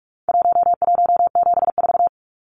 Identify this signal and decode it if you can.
Morse code, heard as a keyed tone, about 44 wpm, 700 Hz. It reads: 1174
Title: Numbers - Contest Style - 4-Digit 45wpm